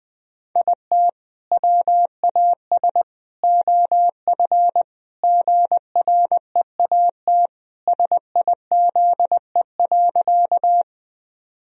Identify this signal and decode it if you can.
Morse code, heard as a keyed tone, about 20 wpm, 700 Hz. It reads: IT WAS OF GREAT SIZE.
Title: Sentences from Top 1000 Words - Review Part 2 - 20wpm